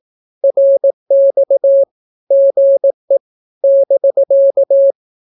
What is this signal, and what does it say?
Morse code, heard as a keyed tone, about 18 wpm, 550 Hz. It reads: RX GE <BK>
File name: naukapodstaw / skroty / skroty